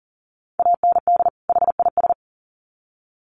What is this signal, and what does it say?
Morse code, heard as a keyed tone, about 40 wpm, 700 Hz. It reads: AND HIS
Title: Sets of 2 Words 40wpm